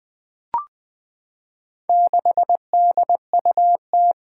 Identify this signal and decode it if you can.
Morse code, heard as a keyed tone, about 20 wpm, 700 Hz. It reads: 6DUT
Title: Four Challenging Letter-Number Combinations 20wpm